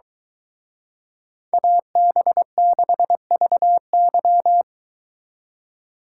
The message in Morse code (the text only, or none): AB6VY